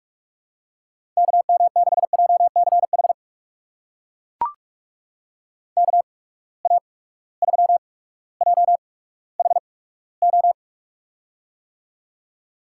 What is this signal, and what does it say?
Morse code, heard as a keyed tone, about 42 wpm, spaced as for 19 wpm, 700 Hz. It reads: KM6JCH KA3JHO
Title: USA Call Signs - 6 Characters - 8x Character Spacing - 45wpm